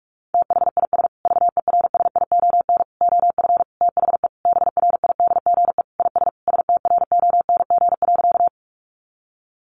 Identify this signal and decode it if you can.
Morse code, heard as a keyed tone, about 44 wpm, 700 Hz. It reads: THIS VERSION OF THE BRIDGE IS STRONG.
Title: Sentences from Top 1400 Words 45wpm